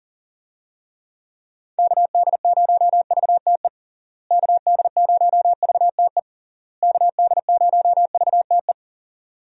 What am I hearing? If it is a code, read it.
Morse code, heard as a keyed tone, about 40 wpm, 700 Hz. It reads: KD0VTE KD0VTE KD0VTE
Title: USA Call Signs - 6 Characters - Repeated 3x - 40wpm